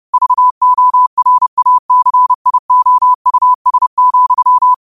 Morse code, heard as a keyed tone, about 30 wpm, 1000 Hz. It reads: VORACIOUS,
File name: qotdfiles / 30wpm